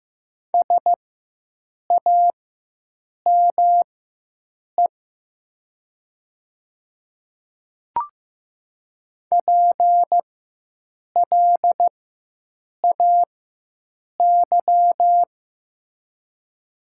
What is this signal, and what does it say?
Morse code, heard as a keyed tone, about 15 wpm, spaced as for 7 wpm, 700 Hz. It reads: SAME PLAY
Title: 4x Character Spacing - Top 200 Words - 15wpm